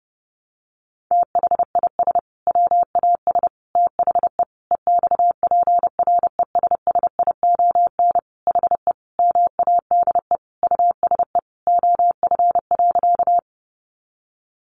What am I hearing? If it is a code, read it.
Morse code, heard as a keyed tone, about 30 wpm, 700 Hz. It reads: THIS WAS THE EXPRESSION HE MADE USE OF.
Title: Sentences from Top 1200 Words 30wpm